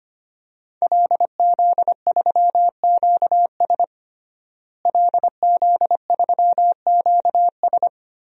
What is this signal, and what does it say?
Morse code, heard as a keyed tone, about 25 wpm, 700 Hz. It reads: LZ3QS LZ3QS
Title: DXCC-100 Call Signs - 5 Characters - Repeated 3x - 25wpm